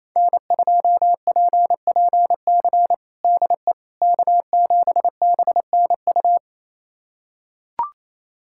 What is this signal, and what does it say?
Morse code, heard as a keyed tone, about 28 wpm, 700 Hz. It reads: N2PPC DE K7BNU
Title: USA Callsign DE Callsign - 5 Characters - 28wpm